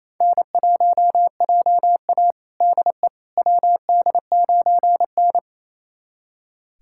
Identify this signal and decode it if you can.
Morse code, heard as a keyed tone, about 28 wpm, 700 Hz. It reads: N1JA DE WD9N
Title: USA Callsign DE Callsign - 4 Characters - 28wpm